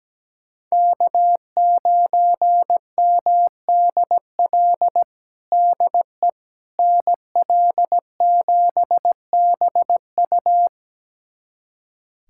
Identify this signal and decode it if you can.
Morse code, heard as a keyed tone, about 17 wpm, 700 Hz. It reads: K9MDL DE NL7BU
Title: USA Callsign DE Callsign - 5 Characters - 17wpm